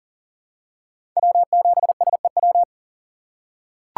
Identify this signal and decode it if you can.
Morse code, heard as a keyed tone, about 37 wpm, 700 Hz. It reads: W7SEW
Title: USA Call Signs - 5 Characters - 6x Character Spacing - 40wpm